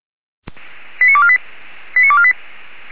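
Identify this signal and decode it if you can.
Five-tone selective call (ZVEI), heard as quick runs of five tones, tones of about 70 ms. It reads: 89248 89248
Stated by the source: Five-tone ZVEI selective call received by SDR, from audio.f85360.feuerwehr-orlamuende.org fire brigade server.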